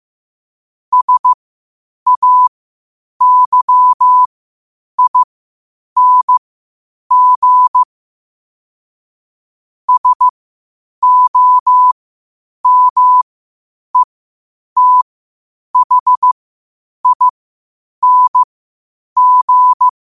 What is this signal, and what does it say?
Morse code, heard as a keyed tone, about 15 wpm, spaced as for 8 wpm, 1000 Hz. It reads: SAYING SOMETHING
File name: qotdfiles / 8wpm